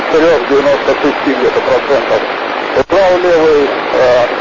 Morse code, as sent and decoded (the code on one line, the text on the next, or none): none